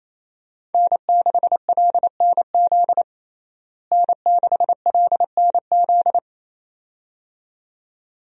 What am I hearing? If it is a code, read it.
Morse code, heard as a keyed tone, about 28 wpm, 700 Hz. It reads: N6LNZ N6LNZ
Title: USA Call Signs - 5 Characters - Repeated 2x - 28wpm